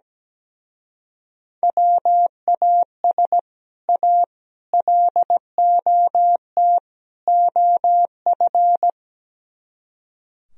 Morse code, heard as a keyed tone, about 17 wpm, 700 Hz. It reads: WAS A LOT OF
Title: Speed-Racing - Sets of 4 Words 17wpm